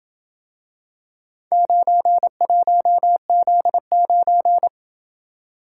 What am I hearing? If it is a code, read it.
Morse code, heard as a keyed tone, about 27 wpm, 700 Hz. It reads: 91Z9